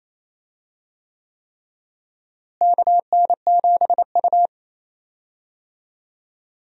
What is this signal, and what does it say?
Morse code, heard as a keyed tone, about 28 wpm, 700 Hz. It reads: KN7U
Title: USA Call Signs - 4 Characters - Repeated 3x - 28wpm